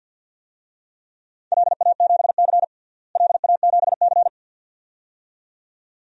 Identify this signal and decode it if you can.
Morse code, heard as a keyed tone, about 52 wpm, 700 Hz. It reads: LA7C LA7C
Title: DXCC-100 Call Signs - 4 Characters - Repeated 2x - 50wpm